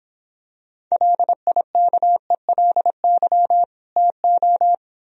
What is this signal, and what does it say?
Morse code, heard as a keyed tone, about 26 wpm, 700 Hz. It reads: LIKELY TO